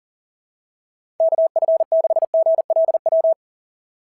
Morse code, heard as a keyed tone, about 38 wpm, 660 Hz. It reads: KF6GLW
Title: SOTA QSO Practice 40wpm